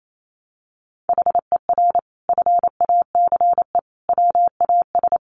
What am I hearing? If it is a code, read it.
Morse code, heard as a keyed tone, about 28 wpm, 700 Hz. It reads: HER FACE WAS
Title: Sentences from Top 1200 Words 28wpm